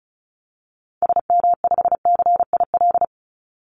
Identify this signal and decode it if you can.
Morse code, heard as a keyed tone, about 34 wpm, 700 Hz. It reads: SM5CIL
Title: Call Signs - 35wpm